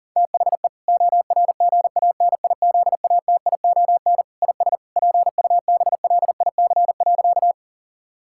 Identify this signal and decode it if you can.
Morse code, heard as a keyed tone, about 40 wpm, 700 Hz. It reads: THE ORGANIZATION IS PUBLIC.
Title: Sentences from Top 1000 Words - Review Part 5 - 40wpm